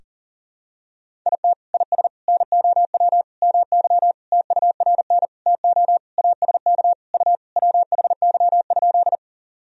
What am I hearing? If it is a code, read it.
Morse code, heard as a keyed tone, about 40 wpm, 700 Hz. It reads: IT IS NOW MY TURN TO ASK U WHY?
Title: Sentences from Top 200 Words and Abbreviations 40wpm